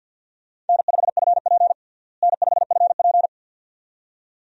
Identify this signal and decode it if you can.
Morse code, heard as a keyed tone, about 46 wpm, 700 Hz. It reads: N5FP N5FP
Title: USA Call Signs - 4 Characters - Repeated 2x - 50wpm